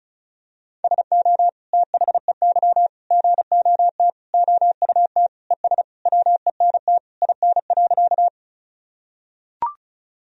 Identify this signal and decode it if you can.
Morse code, heard as a keyed tone, about 36 wpm, 700 Hz. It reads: SO THEY GOT OUT ES WENT IN.
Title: Sentences from Top 300 Words and Abbreviations 35wpm